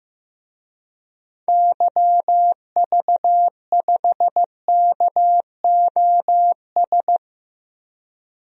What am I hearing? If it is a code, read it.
Morse code, heard as a keyed tone, about 16 wpm, 700 Hz. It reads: YV5KOS
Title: DXCC-100 Call Signs - 6 Characters - Repeated 2x - 15wpm